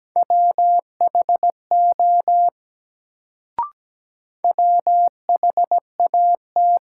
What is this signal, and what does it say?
Morse code, heard as a keyed tone, about 17 wpm, 700 Hz. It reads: WHO WHAT
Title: Sets of 2 Words - New Word Drill - 17wpm